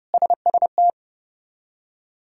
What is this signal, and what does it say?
Morse code, heard as a keyed tone, about 31 wpm, 700 Hz. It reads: SST
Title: K1USN SST QSO Practice - 30wpm